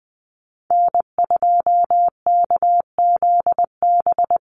Morse code, heard as a keyed tone, about 20 wpm, 700 Hz. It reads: N2KZB
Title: USA Call Signs 20wpm